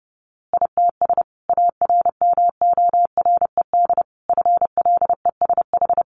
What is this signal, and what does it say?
Morse code, heard as a keyed tone, about 28 wpm, 700 Hz. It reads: ITS ARMORED FLESH
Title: A Sound Of Thunder 30wpm